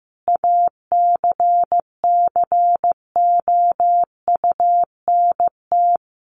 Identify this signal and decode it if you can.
Morse code, heard as a keyed tone, about 15 wpm, 700 Hz. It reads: ACCOUNT
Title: Sentences from Top 900 Words 15wpm